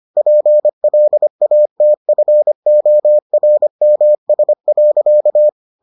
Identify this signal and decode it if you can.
Morse code, heard as a keyed tone, about 25 wpm, 600 Hz. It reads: PLATFORMS.